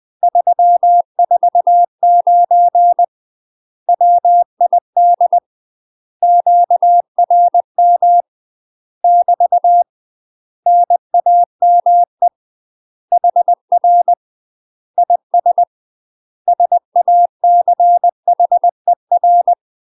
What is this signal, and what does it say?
Morse code, heard as a keyed tone, about 20 wpm, 700 Hz. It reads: 349 WID QRM <BT> NAME HR IS SACHER
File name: UK MASTER QSOs TRAINING / 20WPM / UK MASTER QSOs 0014